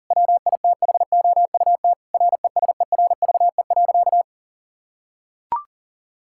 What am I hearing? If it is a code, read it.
Morse code, heard as a keyed tone, about 40 wpm, 700 Hz. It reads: WITHOUT RESERVE.
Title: Sentences from Top 1000 Words - Review Part 6 - 40wpm